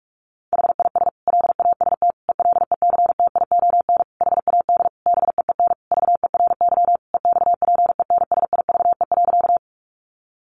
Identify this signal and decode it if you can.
Morse code, heard as a keyed tone, about 44 wpm, 700 Hz. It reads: HIS LAST ELECTION HAD BEEN VERY EXPENSIVE.
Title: Sentences from Top 1500 Words 45wpm